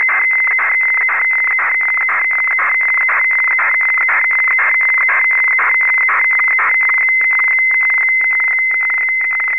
Weather fax (radiofax).